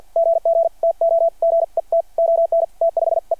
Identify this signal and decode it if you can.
Morse code, heard as a keyed tone, about 53 wpm, spaced as for 45 wpm, 660 Hz. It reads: GG TO GET ON T5E
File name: archived / CRSnet